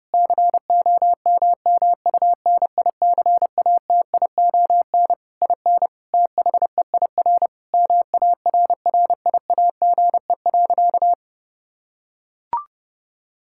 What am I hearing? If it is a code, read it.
Morse code, heard as a keyed tone, about 30 wpm, 700 Hz. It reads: COMMUNICATION IN THEIR MARRIAGE.
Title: Sentences from Top 1000 Words - Review Part 6 - 30wpm